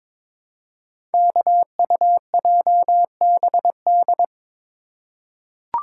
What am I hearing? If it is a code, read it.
Morse code, heard as a keyed tone, about 22 wpm, 690 Hz. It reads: KUJBD
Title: Code Groups - 5 Letter:Numbers - 22wpm